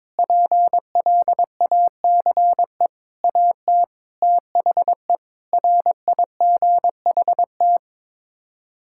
Morse code, heard as a keyed tone, about 22 wpm, 700 Hz. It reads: PLACE AT THE RIGHT
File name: Sets of 9 Words - 22wpm